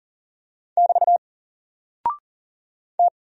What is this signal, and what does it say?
Morse code, heard as a keyed tone, about 40 wpm, 700 Hz. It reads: <BT> T
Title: Single Character (Letter:Number:Punctuation:Prosign) Alt 40wpm